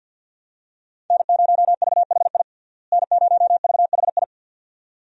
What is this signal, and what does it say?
Morse code, heard as a keyed tone, about 52 wpm, 700 Hz. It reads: N0VHI N0VHI
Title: USA Call Signs - 5 Characters - Repeated 2x - 50wpm